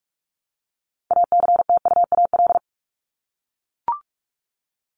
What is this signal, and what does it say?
Morse code, heard as a keyed tone, about 44 wpm, 700 Hz. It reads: ACTUAL
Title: LY Suffix - Top 100 Words with Base Words 45wpm